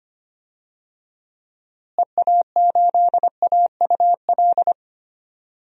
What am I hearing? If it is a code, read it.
Morse code, heard as a keyed tone, about 25 wpm, 700 Hz. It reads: EA8AUL
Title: DXCC-100 Call Signs - 6 Characters - Repeated 3x - 25wpm